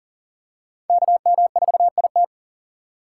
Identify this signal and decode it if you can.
Morse code, heard as a keyed tone, about 40 wpm, 700 Hz. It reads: KM4IT
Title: USA Call Signs - 5 Characters - 2x Character Spacing - 40wpm